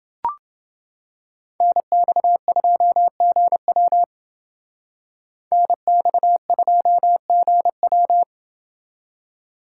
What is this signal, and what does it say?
Morse code, heard as a keyed tone, about 28 wpm, 700 Hz. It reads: NX2GW NX2GW